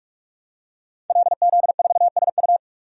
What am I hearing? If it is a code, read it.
Morse code, heard as a keyed tone, about 43 wpm, 700 Hz. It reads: LZ4SU
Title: DXCC-100 Call Signs - 5 Characters - 45wpm